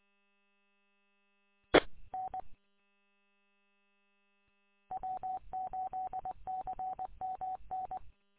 Morse code, heard as a keyed tone, about 20 wpm, 750 Hz. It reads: E N W8CMN
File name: w8cmn- -combined